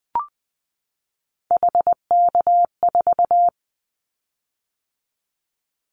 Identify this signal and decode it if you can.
Morse code, heard as a keyed tone, about 20 wpm, 700 Hz. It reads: HK4
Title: Callsign Prefixes for Top 100 DXCC - 20wpm